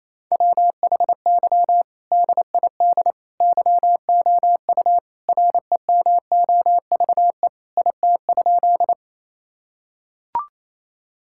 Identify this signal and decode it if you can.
Morse code, heard as a keyed tone, about 28 wpm, 700 Hz. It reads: WHY DID YOU REMOVE IT?
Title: Sentences from Top 1000 Words - Review Part 5 - 28wpm